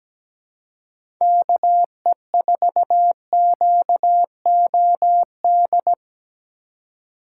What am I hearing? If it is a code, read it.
Morse code, heard as a keyed tone, about 17 wpm, 700 Hz. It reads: KE4QOD
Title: USA Call Signs - 6 Characters - 2x Character Spacing - 17wpm